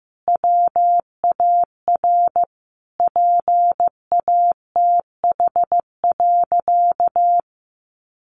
Morse code, heard as a keyed tone, about 15 wpm, 700 Hz. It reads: WAR PATH.
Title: Sentences from Top 1400 Words 15wpm